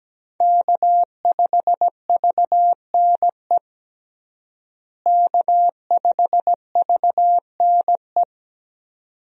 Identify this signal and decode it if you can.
Morse code, heard as a keyed tone, about 17 wpm, 700 Hz. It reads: K5VNE K5VNE
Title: USA Call Signs - 5 Characters - Repeated 2x - 17wpm